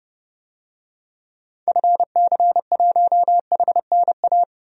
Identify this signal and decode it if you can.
Morse code, heard as a keyed tone, about 30 wpm, 700 Hz. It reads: FC1HNA
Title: DXCC-100 Call Signs - 6 Characters - 30wpm